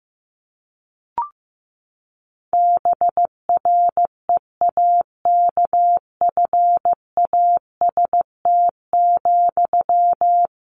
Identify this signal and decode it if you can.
Morse code, heard as a keyed tone, about 15 wpm, 700 Hz. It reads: BREAKFAST,
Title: Sentences from Top 1500 Words 15wpm